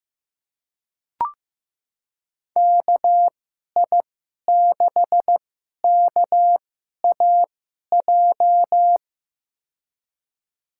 Morse code, heard as a keyed tone, about 15 wpm, spaced as for 11 wpm, 700 Hz. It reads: KI6KAJ
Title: USA Call Signs - 6 Characters - 2x Character Spacing - 15wpm